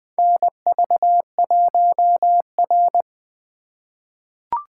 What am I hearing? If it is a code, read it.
Morse code, heard as a keyed tone, about 21 wpm, 700 Hz. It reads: NV1R
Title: USA Call Signs - 4 Characters - 2x Character Spacing - 20wpm